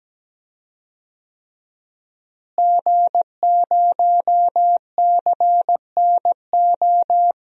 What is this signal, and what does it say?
Morse code, heard as a keyed tone, about 17 wpm, 700 Hz. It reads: G0CNO
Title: DXCC-100 Call Signs - 5 Characters - Repeated 3x - 17wpm